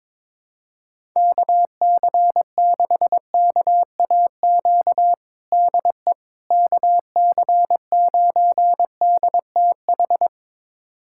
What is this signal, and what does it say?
Morse code, heard as a keyed tone, about 22 wpm, 700 Hz. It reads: KC6KAQ DE KC9DTH
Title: USA Callsign DE Callsign - 6 Characters - 22wpm